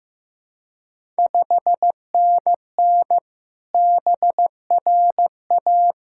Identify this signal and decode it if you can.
Morse code, heard as a keyed tone, about 15 wpm, 700 Hz. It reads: 5NN BRA
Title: Ontario QSO Party Exchange 15wpm